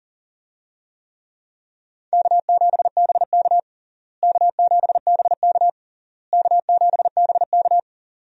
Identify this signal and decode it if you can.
Morse code, heard as a keyed tone, about 40 wpm, 700 Hz. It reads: K7BK K7BK K7BK
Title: USA Call Signs - 4 Characters - Repeated 3x - 40wpm